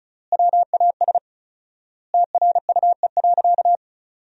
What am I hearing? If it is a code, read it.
Morse code, heard as a keyed tone, about 36 wpm, 700 Hz. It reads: WAS TRUE.
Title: 4x Word-Spacing - Sentences from Top 400 Words - 35wpm